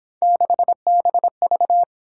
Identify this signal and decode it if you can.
Morse code, heard as a keyed tone, about 26 wpm, 700 Hz. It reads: 6BV
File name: Speed-Racing - Three Letter-Number Combinations - 22wpm